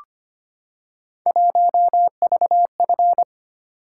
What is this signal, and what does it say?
Morse code, heard as a keyed tone, about 25 wpm, 700 Hz. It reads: 1VF
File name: Three Letter-Number Combinations - 25wpm